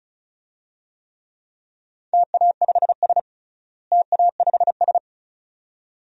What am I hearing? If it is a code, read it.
Morse code, heard as a keyed tone, about 36 wpm, 700 Hz. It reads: TA5S TA5S
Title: DXCC-100 Call Signs - 4 Characters - Repeated 2x - 35wpm